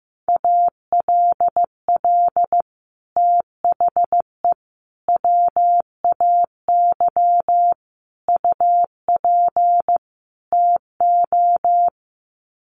Morse code, heard as a keyed tone, about 15 wpm, 700 Hz. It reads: ALL THE WAY UP TO
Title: Sets of 5 Words Encore 15wpm